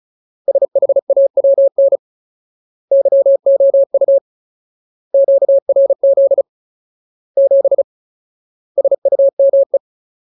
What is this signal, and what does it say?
Morse code, heard as a keyed tone, about 34 wpm, 550 Hz. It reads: SHAWN YOU QRZ 7 SUME
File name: Ultimate Word List / 35WPM Ultimate